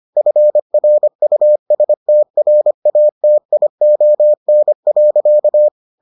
Morse code, heard as a keyed tone, about 25 wpm, 600 Hz. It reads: FRUSTRATION.